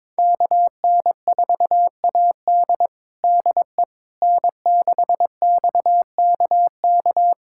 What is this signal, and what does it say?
Morse code, heard as a keyed tone, about 22 wpm, 700 Hz. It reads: KN4AD DE N6XKK
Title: USA Callsign DE Callsign - 5 Characters - 22wpm